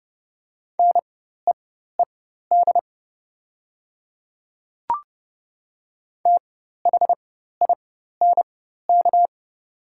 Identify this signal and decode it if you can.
Morse code, heard as a keyed tone, about 29 wpm, spaced as for 14 wpm, 700 Hz. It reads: NEED THINK
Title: 4x Character Spacing - Top 200 Words - 30wpm